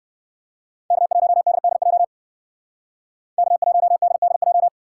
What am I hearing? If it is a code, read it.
Morse code, heard as a keyed tone, about 67 wpm, 700 Hz. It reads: K1DDJ K1DDJ